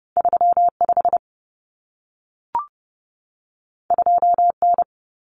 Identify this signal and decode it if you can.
Morse code, heard as a keyed tone, about 30 wpm, 700 Hz. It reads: 35 2N
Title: Common Cut Numbers - 2-Digit 30wpm